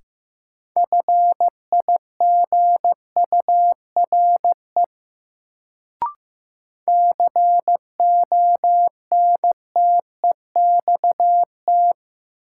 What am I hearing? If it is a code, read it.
Morse code, heard as a keyed tone, about 15 wpm, 700 Hz. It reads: FIGURE CONTEXT
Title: Sets of 4 Words - New Word Drill - 15wpm